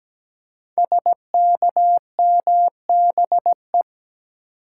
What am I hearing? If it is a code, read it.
Morse code, heard as a keyed tone, about 17 wpm, 700 Hz. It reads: SKMBE
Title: Code Groups - 5 Letters - 17wpm